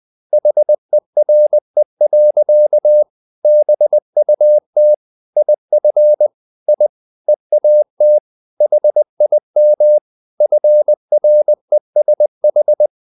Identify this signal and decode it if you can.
Morse code, heard as a keyed tone, about 20 wpm, 600 Hz. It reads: HERE. BUT IF I EAT HIM FRESH
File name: the old man and the sea / the old man and the sea part 1 20wpm